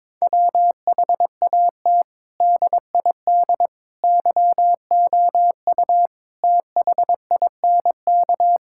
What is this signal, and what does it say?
Morse code, heard as a keyed tone, about 22 wpm, 700 Hz. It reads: WHAT DID YOU THINK
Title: Sentences from Top 1000 Words - Review Part 5 - 22wpm